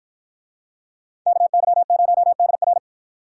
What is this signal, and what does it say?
Morse code, heard as a keyed tone, about 53 wpm, 700 Hz. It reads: KY0DR